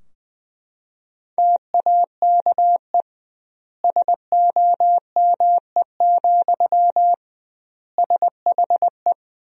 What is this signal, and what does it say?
Morse code, heard as a keyed tone, about 21 wpm, 700 Hz. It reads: TAKE SOME, SHE
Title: 2x Word-Spacing - Sentences from Top 200 Words - 20wpm